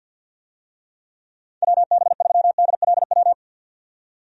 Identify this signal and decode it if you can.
Morse code, heard as a keyed tone, about 53 wpm, 700 Hz. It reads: WB3DLW
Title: USA Call Signs - 6 Characters - 50wpm